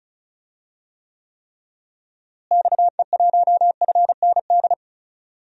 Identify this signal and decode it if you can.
Morse code, heard as a keyed tone, about 36 wpm, 700 Hz. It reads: XE1FND
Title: DXCC-100 Call Signs - 6 Characters - Repeated 3x - 35wpm